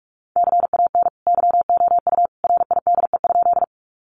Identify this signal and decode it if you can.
Morse code, heard as a keyed tone, about 44 wpm, 700 Hz. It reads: CAN YOU RIDE?
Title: Sentences from Top 1200 Words 45wpm